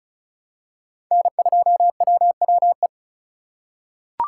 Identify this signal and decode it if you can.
Morse code, heard as a keyed tone, about 34 wpm, 700 Hz. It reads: N2WWE
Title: USA Call Signs - 5 Characters - 4x Character Spacing - 35wpm